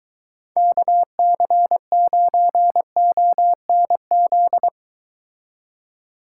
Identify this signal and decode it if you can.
Morse code, heard as a keyed tone, about 22 wpm, 700 Hz. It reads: KC9ONZ